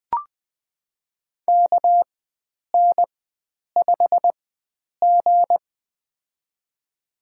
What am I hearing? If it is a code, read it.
Morse code, heard as a keyed tone, about 21 wpm, spaced as for 9 wpm, 700 Hz. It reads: KN5G
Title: USA Call Signs - 4 Characters - 4x Character Spacing - 20wpm